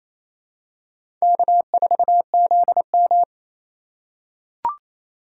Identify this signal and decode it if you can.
Morse code, heard as a keyed tone, about 28 wpm, 700 Hz. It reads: K4ZM
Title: USA Call Signs - 4 Characters - 3x Character Spacing - 28wpm